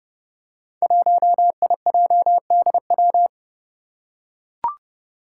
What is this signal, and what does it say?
Morse code, heard as a keyed tone, about 30 wpm, 700 Hz. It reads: 1IJDW
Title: Five Challenging Letter-Number Combinations 30wpm